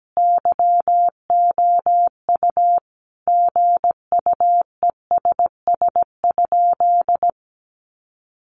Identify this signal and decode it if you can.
Morse code, heard as a keyed tone, about 17 wpm, 700 Hz. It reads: YOU GUESS?
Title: Sentences from Top 1500 Words 17wpm